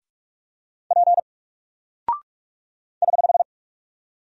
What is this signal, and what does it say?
Morse code, heard as a keyed tone, about 44 wpm, 700 Hz. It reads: P <HH>
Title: Single Character (Letter:Number:Punctuation:Prosign) 45wpm